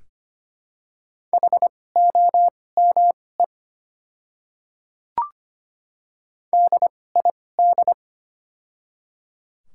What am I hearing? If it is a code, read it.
Morse code, heard as a keyed tone, about 25 wpm, spaced as for 18 wpm, 700 Hz. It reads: HOME DID
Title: 2x Character Spacing - Top 500 Words - 25wpm